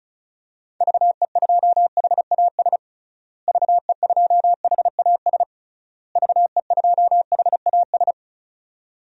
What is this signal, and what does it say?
Morse code, heard as a keyed tone, about 36 wpm, 700 Hz. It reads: VE2HAS VE2HAS VE2HAS
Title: DXCC-100 Call Signs - 6 Characters - Repeated 3x - 35wpm